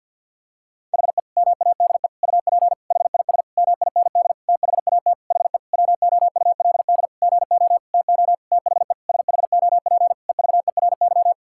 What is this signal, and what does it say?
Morse code, heard as a keyed tone, about 50 wpm, 700 Hz. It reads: HE MADE UP HIS MIND THAT HE WOULD GO TO THE SHOP EVERY
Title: Sentences from Top 1600 Words 50wpm